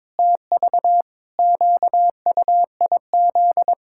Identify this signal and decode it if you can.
Morse code, heard as a keyed tone, about 21 wpm, 700 Hz. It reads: TV QUIZ
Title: Pangram Challenge v2 - 22wpm